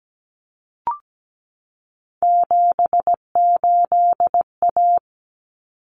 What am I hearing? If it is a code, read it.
Morse code, heard as a keyed tone, about 17 wpm, 700 Hz. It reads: 78A
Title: Common Cut Numbers - 3-Digit 17wpm